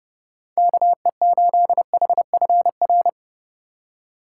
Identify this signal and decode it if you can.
Morse code, heard as a keyed tone, about 30 wpm, 700 Hz. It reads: KE8HFR
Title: USA Call Signs - 6 Characters - 3x Character Spacing - 30wpm